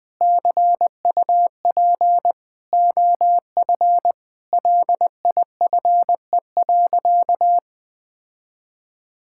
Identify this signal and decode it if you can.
Morse code, heard as a keyed tone, about 20 wpm, 700 Hz. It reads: CUP OF LIFE.
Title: Sentences from Top 1600 Words 20wpm